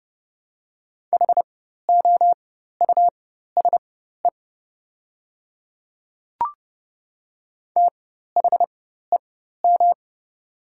Morse code, heard as a keyed tone, about 30 wpm, spaced as for 14 wpm, 700 Hz. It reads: HOUSE THEM
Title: 4x Character Spacing - Top 200 Words - 30wpm